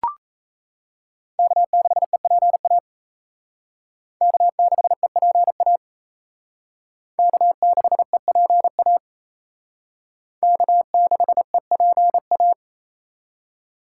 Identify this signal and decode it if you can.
Morse code, heard as a keyed tone, about 36 wpm, 700 Hz. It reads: K6EPA K6EPA K6EPA K6EPA